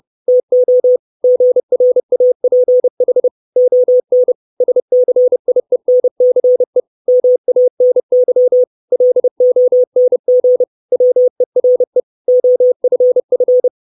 Morse code, heard as a keyed tone, about 30 wpm, 500 Hz. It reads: TO GRAPH ON SCIENCE MANY LONG WERE OFF
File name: 632 Words 30WPM